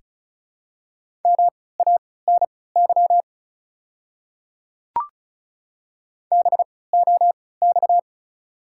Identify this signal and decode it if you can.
Morse code, heard as a keyed tone, about 36 wpm, spaced as for 20 wpm, 690 Hz. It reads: MANY BOX
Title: 3x Character Spacing - Top 500 Words - 35wpm